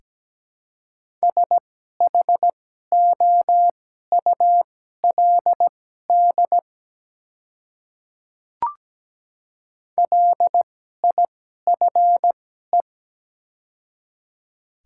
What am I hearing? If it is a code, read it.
Morse code, heard as a keyed tone, about 17 wpm, spaced as for 12 wpm, 690 Hz. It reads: SHOULD LIFE
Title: 2x Character Spacing - Top 300 Words - 17wpm